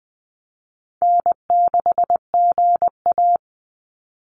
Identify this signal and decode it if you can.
Morse code, heard as a keyed tone, about 20 wpm, 700 Hz. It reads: N6GA
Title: Easy Call Signs Repeated 2x - 20wpm